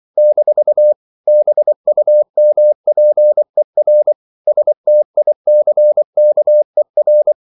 Morse code, heard as a keyed tone, about 24 wpm, 600 Hz. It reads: - BUMPER STICKER
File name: CW / Practice / EnglishQuotations0224